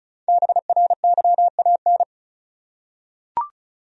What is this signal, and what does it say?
Morse code, heard as a keyed tone, about 36 wpm, 700 Hz. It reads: BRYAN
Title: USA Common Names - 35wpm